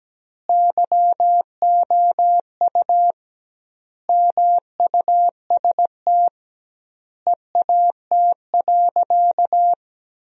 Morse code, heard as a keyed tone, about 17 wpm, 700 Hz. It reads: YOU MUST EAT.
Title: 2x Word-Spacing - Sentences from Top 300 Words - 17wpm